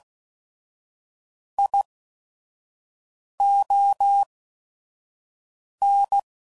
Morse code, heard as a keyed tone, about 16 wpm, spaced as for 5 wpm, 800 Hz. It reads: ION
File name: CW Code Groups @5WPM / Technical Subjects / AND Gate